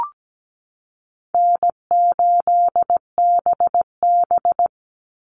Morse code, heard as a keyed tone, about 17 wpm, 700 Hz. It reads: N8BB
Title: NA SOTA Chaser Callsigns 17wpm